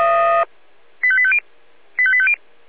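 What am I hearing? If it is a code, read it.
Five-tone selective call (ZVEI), heard as quick runs of five tones, tones of about 70 ms. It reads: 85850 85850
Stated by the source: ZVEI five-tone selective call received on an SDR.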